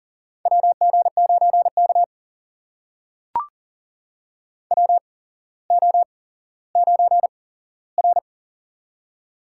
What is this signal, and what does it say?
Morse code, heard as a keyed tone, about 42 wpm, spaced as for 17 wpm, 700 Hz. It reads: WG9K WO9R
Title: USA Call Signs - 4 Characters - 8x Character Spacing - 40wpm